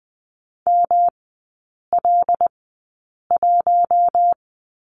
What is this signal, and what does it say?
Morse code, heard as a keyed tone, about 20 wpm, spaced as for 8 wpm, 700 Hz. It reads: ML1
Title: 107 Lesson 18e - Letter m - Three Characters 20wpm